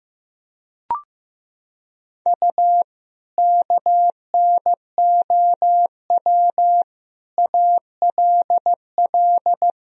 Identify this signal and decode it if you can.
Morse code, heard as a keyed tone, about 15 wpm, 700 Hz. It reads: U KNOW ALL
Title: Sentences from Top 300 Words and Abbreviations 15wpm